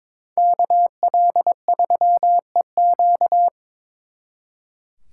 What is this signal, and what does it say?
Morse code, heard as a keyed tone, about 26 wpm, 700 Hz. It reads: KL3EQ
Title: Speed Racing - USA 2x2 Call Signs 22wpm